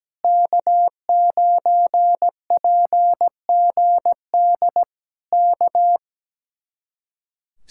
Morse code, heard as a keyed tone, about 17 wpm, 700 Hz. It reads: K9PGD K
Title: Speed-Racing Common QSO Phrases - 17wpm